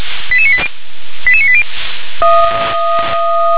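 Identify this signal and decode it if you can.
Five-tone selective call (ZVEI), heard as quick runs of five tones, tones of about 70 ms. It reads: 88980 88980
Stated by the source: ZVEI five-tone selective call 88774 received by SDR.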